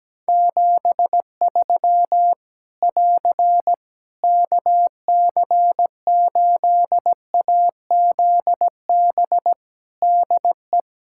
Morse code, heard as a keyed tone, about 17 wpm, 700 Hz. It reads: 73 <AR> KC8AZB DE
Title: Practice QSO - Exchange 4 - Ending - 17wpm